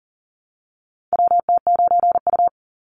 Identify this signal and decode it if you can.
Morse code, heard as a keyed tone, about 40 wpm, 700 Hz. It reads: WT9U
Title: Call Signs Repeated 3x - 40wpm